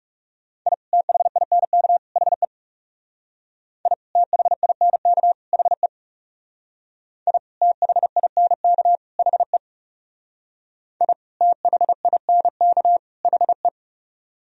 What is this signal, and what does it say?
Morse code, heard as a keyed tone, about 37 wpm, 700 Hz. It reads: I THINK HE I THINK HE I THINK HE I THINK HE
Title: Speed-Racing - Sets of 3 Words 30wpm